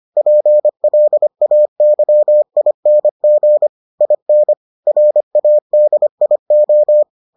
Morse code, heard as a keyed tone, about 25 wpm, 600 Hz. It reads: PLAYING IN RADIO